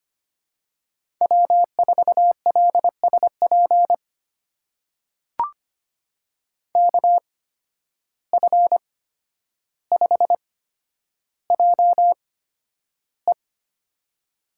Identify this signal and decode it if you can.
Morse code, heard as a keyed tone, about 25 wpm, spaced as for 11 wpm, 700 Hz. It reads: W4LSP KF5JE
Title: USA Call Signs - 5 Characters - 8x Character Spacing - 25wpm